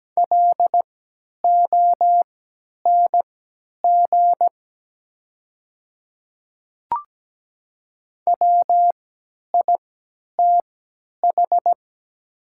Morse code, heard as a keyed tone, about 17 wpm, spaced as for 10 wpm, 700 Hz. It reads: LONG WITH
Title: 3x Character Spacing - Top 100 Words - 17wpm